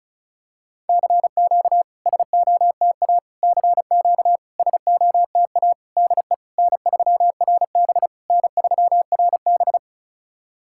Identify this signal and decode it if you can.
Morse code, heard as a keyed tone, about 36 wpm, 700 Hz. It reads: CQ SOTA CQ SOTA DE N3RB N3RB
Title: SOTA QSO Practice 35wpm